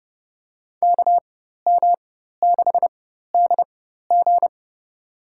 Morse code, heard as a keyed tone, about 30 wpm, spaced as for 14 wpm, 700 Hz. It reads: KM6DG
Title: USA Call Signs - 5 Characters - 4x Character Spacing - 30wpm